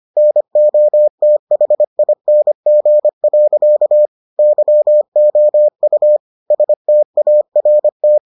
Morse code, heard as a keyed tone, about 25 wpm, 600 Hz. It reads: NOTHING. YOU START